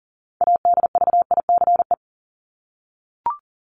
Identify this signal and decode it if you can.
Morse code, heard as a keyed tone, about 40 wpm, 700 Hz. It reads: ADVICE
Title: Top 1001-1250 Words 40wpm